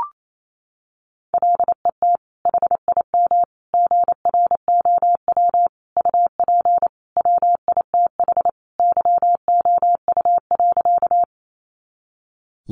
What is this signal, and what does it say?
Morse code, heard as a keyed tone, about 28 wpm, 700 Hz. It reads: LET HIM GROW UP WITH YOU.
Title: Sentences from Top 300 Words 28wpm